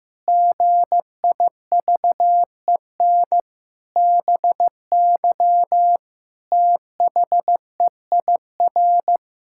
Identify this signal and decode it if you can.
Morse code, heard as a keyed tone, about 15 wpm, 700 Hz. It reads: GIVEN BY THEIR
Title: Pangram Challenge v2 - 15wpm